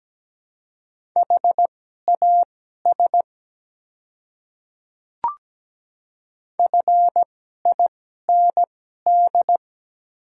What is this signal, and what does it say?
Morse code, heard as a keyed tone, about 17 wpm, spaced as for 12 wpm, 690 Hz. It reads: HAS FIND
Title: 2x Character Spacing - Top 100 Words - 17wpm